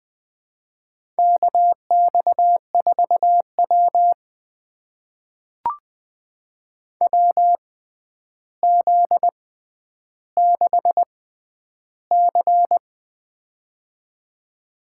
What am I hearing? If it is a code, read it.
Morse code, heard as a keyed tone, about 20 wpm, spaced as for 10 wpm, 700 Hz. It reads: KX4W WZ6C
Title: USA Call Signs - 4 Characters - 6x Character Spacing - 20wpm